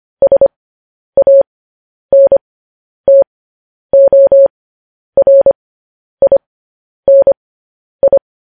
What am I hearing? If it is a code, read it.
Morse code, heard as a keyed tone, about 25 wpm, spaced as for 10 wpm, 550 Hz. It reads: SANTORINI
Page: News Headlines in Morse Code at 10 WPM